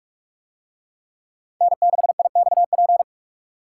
Morse code, heard as a keyed tone, about 42 wpm, 700 Hz. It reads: N6IXP